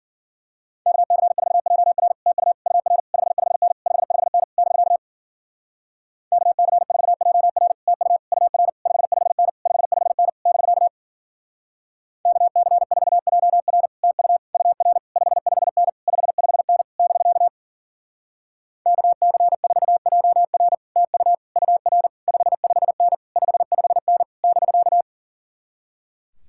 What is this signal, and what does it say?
Morse code, heard as a keyed tone, about 49 wpm, 700 Hz. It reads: KC4JR TU UR 55N 55N <BK> KC4JR TU UR 55N 55N <BK> KC4JR TU UR 55N 55N <BK> KC4JR TU UR 55N 55N <BK>
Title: Speed-Racing Common QSO Phrases - 40wpm